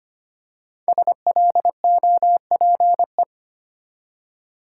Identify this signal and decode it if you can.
Morse code, heard as a keyed tone, about 25 wpm, 700 Hz. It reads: SLOPE
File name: 5 Letter Words - First 3 Letters Given - 25wpm